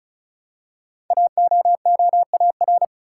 Morse code, heard as a keyed tone, about 36 wpm, 700 Hz. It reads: AOOAR
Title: Code Groups - 5 Letter:Numbers - 35wpm